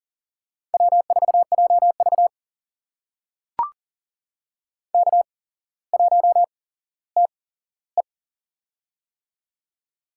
Morse code, heard as a keyed tone, about 38 wpm, spaced as for 17 wpm, 700 Hz. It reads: W4JV K1TE
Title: USA Call Signs - 4 Characters - 8x Character Spacing - 40wpm